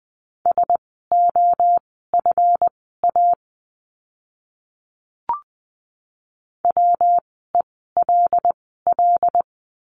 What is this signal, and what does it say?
Morse code, heard as a keyed tone, about 20 wpm, spaced as for 14 wpm, 700 Hz. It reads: SOFA WELL
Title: 139 Lesson 23f - Letter f - Words 20wpm